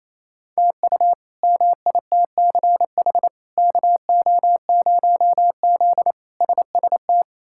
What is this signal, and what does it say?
Morse code, heard as a keyed tone, about 28 wpm, 700 Hz. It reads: TU MITCH KO0Z SST
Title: K1USN SST QSO Practice - 28wpm